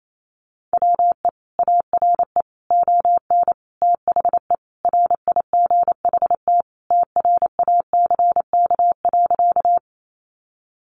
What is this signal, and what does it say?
Morse code, heard as a keyed tone, about 28 wpm, 700 Hz. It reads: WE ARE ON THE RIGHT TRACK.
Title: Sentences from Top 1100 Words 28wpm